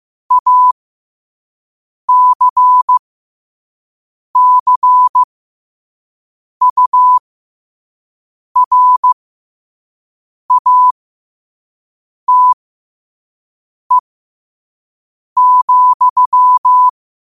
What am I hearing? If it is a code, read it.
Morse code, heard as a keyed tone, about 15 wpm, spaced as for 5 wpm, 1000 Hz. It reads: ACCURATE,
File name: qotdfiles / 5wpm